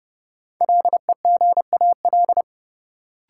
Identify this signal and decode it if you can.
Morse code, heard as a keyed tone, about 31 wpm, 700 Hz. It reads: LEGAL
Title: Sentences from Top 1200 Words - New Word Drill Alt - 30wpm